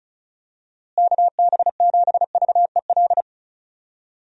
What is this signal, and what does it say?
Morse code, heard as a keyed tone, about 36 wpm, 700 Hz. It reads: KB7VEL
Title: USA Call Signs - 6 Characters - Repeated 3x - 35wpm